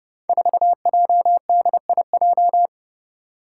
Morse code, heard as a keyed tone, about 30 wpm, 700 Hz. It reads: 4JDIJ
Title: Five Challenging Letter-Number Combinations 30wpm